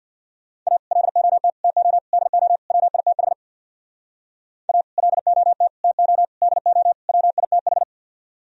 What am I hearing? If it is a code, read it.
Morse code, heard as a keyed tone, about 64 wpm, 700 Hz. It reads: A LOT TO DO WITH A LOT TO DO WITH